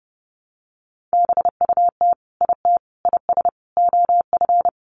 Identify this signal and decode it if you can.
Morse code, heard as a keyed tone, about 30 wpm, 700 Hz. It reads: BUT IT IS OF
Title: Sentences from Top 1200 Words 30wpm